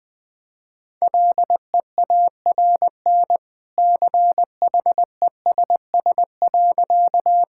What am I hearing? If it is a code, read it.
Morse code, heard as a keyed tone, about 20 wpm, 700 Hz. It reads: LEARN CHESS.
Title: Easy Sentences with Hints - 20wpm